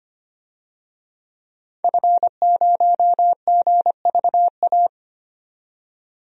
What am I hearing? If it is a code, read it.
Morse code, heard as a keyed tone, about 25 wpm, 700 Hz. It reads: F0GVA
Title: DXCC-100 Call Signs - 5 Characters - 25wpm